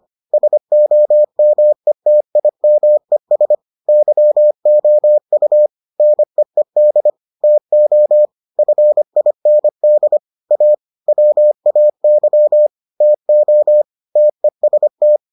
Morse code, heard as a keyed tone, about 25 wpm, 600 Hz. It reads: SOMETIMES YOU NEED TO FIND A WAY TO TEST